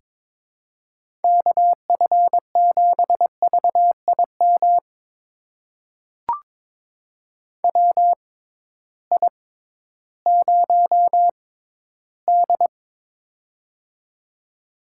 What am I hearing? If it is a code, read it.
Morse code, heard as a keyed tone, about 22 wpm, spaced as for 13 wpm, 700 Hz. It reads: KF7VIM WI0D
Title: USA Call Signs - 6x Character Spacing - 22wpm